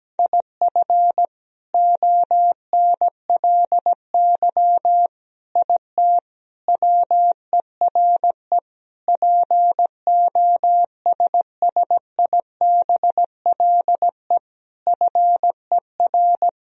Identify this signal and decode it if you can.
Morse code, heard as a keyed tone, about 17 wpm, 700 Hz. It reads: IF ONLY IT WERE POSSIBLE FER
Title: Sentences from Top 500 Words and Abbreviations 17wpm